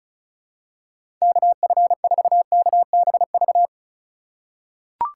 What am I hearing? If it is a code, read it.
Morse code, heard as a keyed tone, about 36 wpm, 700 Hz. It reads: KF4KBV
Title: USA Call Signs - 6 Characters - 2x Character Spacing - 35wpm